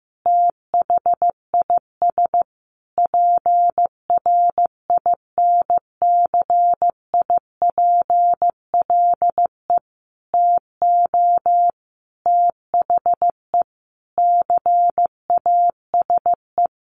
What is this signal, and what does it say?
Morse code, heard as a keyed tone, about 15 wpm, 700 Hz. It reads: THIS PRINCIPLE TO THE CASE
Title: Sentences from Top 1400 Words 15wpm